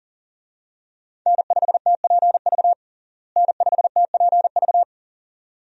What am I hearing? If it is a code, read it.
Morse code, heard as a keyed tone, about 40 wpm, 700 Hz. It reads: N5TPV N5TPV
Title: USA Call Signs - 5 Characters - Repeated 2x - 40wpm